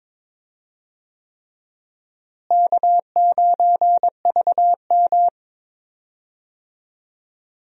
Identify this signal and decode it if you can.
Morse code, heard as a keyed tone, about 21 wpm, 700 Hz. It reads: K9VM
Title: USA Call Signs - 4 Characters - Repeated 3x - 22wpm